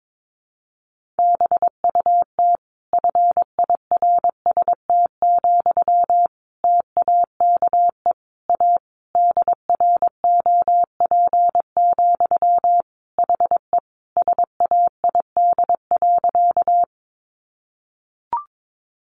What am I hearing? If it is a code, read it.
Morse code, heard as a keyed tone, about 22 wpm, 700 Hz. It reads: BUT FIRST, TAKE A DROP, HE SAID.
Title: Sentences from Top 1400 Words 22wpm